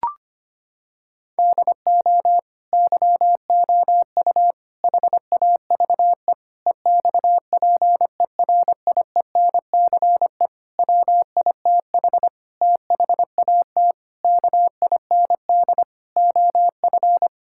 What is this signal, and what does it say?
Morse code, heard as a keyed tone, about 25 wpm, 700 Hz. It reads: DO YOU HAVE EXPERIENCE WITH THAT KIND OF
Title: Sentences from Top 1000 Words - Review Part 5 - 25wpm